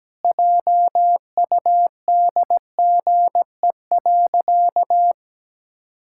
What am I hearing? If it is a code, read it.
Morse code, heard as a keyed tone, about 17 wpm, 700 Hz. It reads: JUDGE.
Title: Pangram Challenge v2 - 17wpm